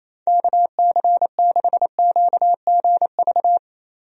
Morse code, heard as a keyed tone, about 28 wpm, 700 Hz. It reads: KC6QGV
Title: USA Call Signs - 6 Characters - Repeated 3x - 28wpm